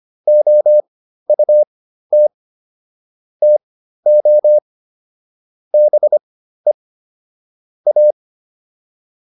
Morse code, heard as a keyed tone, about 25 wpm, spaced as for 13 wpm, 600 Hz. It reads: OUT TO BE A